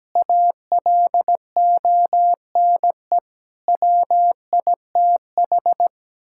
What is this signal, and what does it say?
Morse code, heard as a keyed tone, about 17 wpm, 700 Hz. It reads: ALONE WITH
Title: Sentences from Top 1000 Words - Review Part 5 - 17wpm